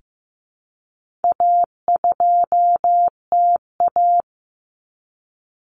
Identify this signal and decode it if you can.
Morse code, heard as a keyed tone, about 15 wpm, 700 Hz. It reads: A2TA
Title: Common Cut Numbers - 4-Digit 15wpm